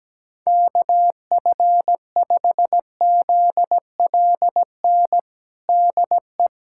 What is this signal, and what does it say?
Morse code, heard as a keyed tone, about 17 wpm, 700 Hz. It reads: KF5ZLN DE
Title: USA Callsign DE Callsign 17wpm